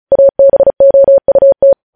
Morse code, heard as a keyed tone, about 36 wpm, 550 Hz. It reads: ABOUT
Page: news headlines in morse code at 35 wpm